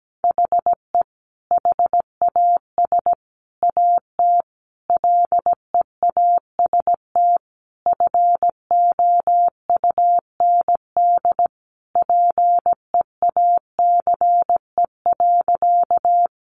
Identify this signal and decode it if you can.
Morse code, heard as a keyed tone, about 17 wpm, 700 Hz. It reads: HE HAS AT LEAST FOUND PEACE.
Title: Sentences from Top 1500 Words 17wpm